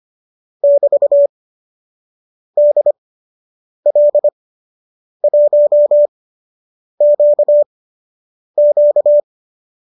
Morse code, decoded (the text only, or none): <BT> DL1QQ